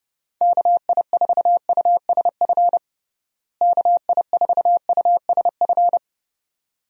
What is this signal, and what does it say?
Morse code, heard as a keyed tone, about 30 wpm, 700 Hz. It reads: KI4USF KI4USF
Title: USA Call Signs - 6 Characters - Repeated 2x - 30wpm